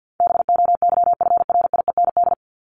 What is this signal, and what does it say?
Morse code, heard as a keyed tone, about 50 wpm, 700 Hz. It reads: BOYFRIEND
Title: Top 501-750 Words 50wpm